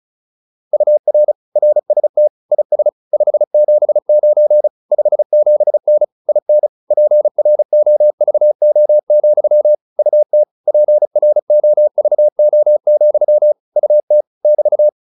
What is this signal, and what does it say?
Morse code, heard as a keyed tone, about 34 wpm, 600 Hz. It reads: UR RST IS 579 57N IN PROVO, UT PROVO, UT <BT>